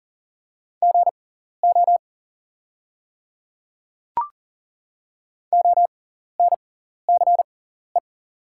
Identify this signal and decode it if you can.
Morse code, heard as a keyed tone, about 37 wpm, spaced as for 14 wpm, 700 Hz. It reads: GO ONCE
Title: 6x Character Spacing - Top 300 Words - 40wpm